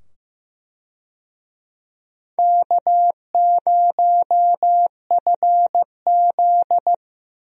Morse code, heard as a keyed tone, about 15 wpm, 700 Hz. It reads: K0FZ
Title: USA Call Signs - 4 Characters - 15wpm